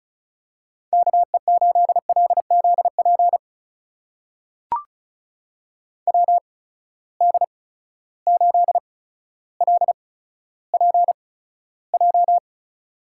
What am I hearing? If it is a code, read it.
Morse code, heard as a keyed tone, about 36 wpm, spaced as for 15 wpm, 700 Hz. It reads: KE8LZP WD8LPJ
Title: USA Call Signs - 6 Characters - 8x Character Spacing - 35wpm